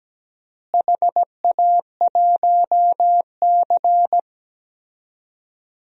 Morse code, heard as a keyed tone, about 17 wpm, 700 Hz. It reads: HA1C
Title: DXCC-100 Call Signs - 4 Characters - Repeated 3x - 17wpm